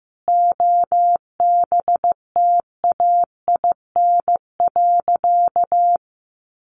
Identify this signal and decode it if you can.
Morse code, heard as a keyed tone, about 15 wpm, 700 Hz. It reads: OBTAIN.
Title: Sentences from Top 1300 Words 15wpm